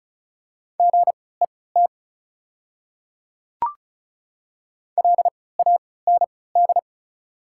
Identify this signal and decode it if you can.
Morse code, heard as a keyed tone, about 36 wpm, spaced as for 20 wpm, 690 Hz. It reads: GET LAND
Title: 3x Character Spacing - Top 200 Words - 35wpm